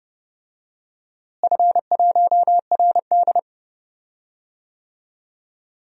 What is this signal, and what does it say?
Morse code, heard as a keyed tone, about 30 wpm, 700 Hz. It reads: F1RD
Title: DXCC-100 Call Signs - 4 Characters - Repeated 3x - 30wpm